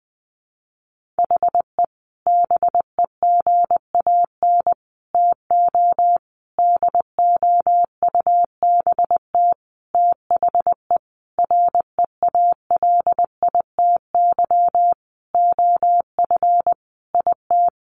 Morse code, heard as a keyed tone, about 20 wpm, 700 Hz. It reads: HE BEGAN TO DOUBT THE REALITY OF IT
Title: Sentences from Top 1500 Words 20wpm